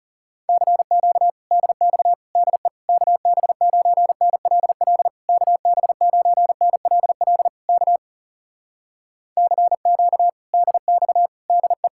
Morse code, heard as a keyed tone, about 37 wpm, 700 Hz. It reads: CQ DX DE KB9NLL KB9NLL K CQ DX DE
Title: Speed-Racing Common QSO Phrases Encore - 30wpm